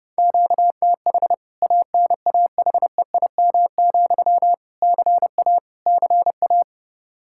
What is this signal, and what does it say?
Morse code, heard as a keyed tone, about 32 wpm, 700 Hz. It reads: QTH ANAHEIM, CA CA
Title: K1USN SST QSO Practice 30wpm